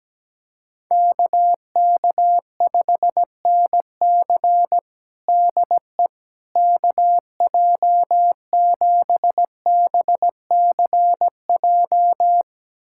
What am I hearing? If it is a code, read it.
Morse code, heard as a keyed tone, about 17 wpm, 700 Hz. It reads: KK5NC DE KJ7BCJ
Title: USA Callsign DE Callsign 17wpm